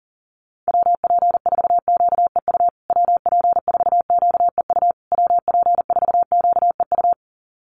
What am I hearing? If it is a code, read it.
Morse code, heard as a keyed tone, about 40 wpm, 700 Hz. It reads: WP4QEU WP4QEU WP4QEU
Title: USA Call Signs Repeated 3x 40wpm